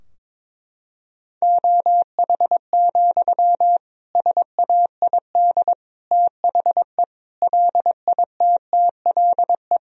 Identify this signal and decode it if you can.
Morse code, heard as a keyed tone, about 22 wpm, 700 Hz. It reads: OH, SAID THE LITTLE
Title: Sentences from Top 1000 Words - Review Part 2 - 22wpm